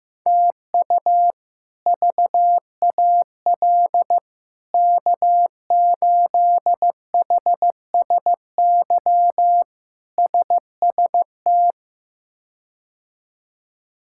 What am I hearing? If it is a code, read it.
Morse code, heard as a keyed tone, about 16 wpm, 700 Hz. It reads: TU VAL K8HSY SST
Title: K1USN SST QSO Practice - 15wpm